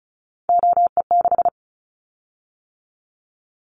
Morse code, heard as a keyed tone, about 36 wpm, 700 Hz. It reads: OE6
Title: Callsign Prefixes for Top 50 DXCC - 35wpm